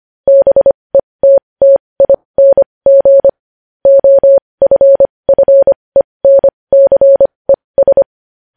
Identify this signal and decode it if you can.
Morse code, heard as a keyed tone, about 25 wpm, spaced as for 20 wpm, 550 Hz. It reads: BETTING OFFENCES
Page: news-headlines-in-morse-code-at-20-wpm